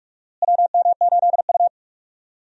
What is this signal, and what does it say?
Morse code, heard as a keyed tone, about 45 wpm, 700 Hz. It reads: WM8U
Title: USA Call Signs - 4 Characters - Repeated 3x - 45wpm